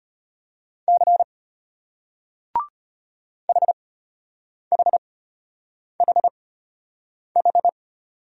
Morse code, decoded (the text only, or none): C HHHH